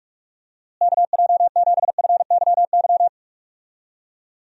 Morse code, heard as a keyed tone, about 46 wpm, 700 Hz. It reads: KJ7FYY
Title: USA Call Signs - 6 Characters - Repeated 3x - 45wpm